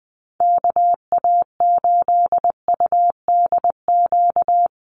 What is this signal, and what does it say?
Morse code, heard as a keyed tone, about 19 wpm, 700 Hz. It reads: KA8UDQ
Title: USA Call Signs Repeated 3x 20wpm